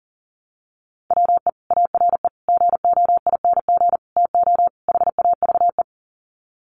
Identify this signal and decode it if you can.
Morse code, heard as a keyed tone, about 40 wpm, 700 Hz. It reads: WE ARE GOING TO HAVE
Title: Sets of 5 Words Encore 40wpm